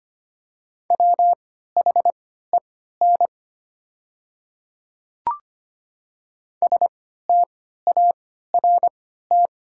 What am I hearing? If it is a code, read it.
Morse code, heard as a keyed tone, about 25 wpm, spaced as for 14 wpm, 700 Hz. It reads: WHEN START
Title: 3x Character Spacing - Top 300 Words - 25wpm